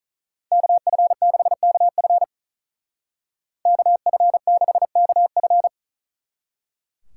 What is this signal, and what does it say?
Morse code, heard as a keyed tone, about 44 wpm, 700 Hz. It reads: KF6KF KF6KF